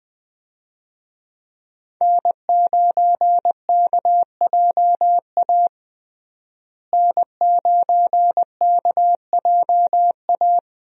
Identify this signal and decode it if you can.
Morse code, heard as a keyed tone, about 20 wpm, 700 Hz. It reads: N9KJA N9KJA
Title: USA Call Signs - 5 Characters - Repeated 2x - 20wpm